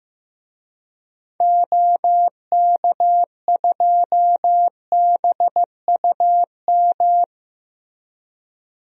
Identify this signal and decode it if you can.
Morse code, heard as a keyed tone, about 15 wpm, 700 Hz. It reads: OK2BUM
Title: DXCC-100 Call Signs - 6 Characters - 15wpm